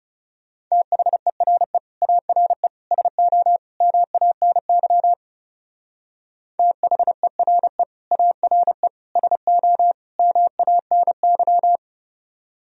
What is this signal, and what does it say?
Morse code, heard as a keyed tone, about 38 wpm, 700 Hz. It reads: THERE ARE SO MANY THERE ARE SO MANY